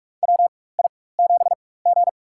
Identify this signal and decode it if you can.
Morse code, heard as a keyed tone, about 42 wpm, spaced as for 20 wpm, 700 Hz. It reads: WI7G
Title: USA Call Signs - 4 Characters - 4x Character Spacing - 45wpm